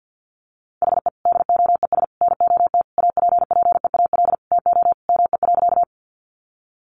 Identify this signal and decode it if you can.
Morse code, heard as a keyed tone, about 50 wpm, 700 Hz. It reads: HE DOES NOT APPEAL TO ME.
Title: Sentences from Top 1300 Words 50wpm